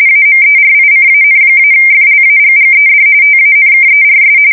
RTTY, about 45.45 baud. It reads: AMERICAN-FLAGGED TANKERS
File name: fsk170 / USS-New-Jersey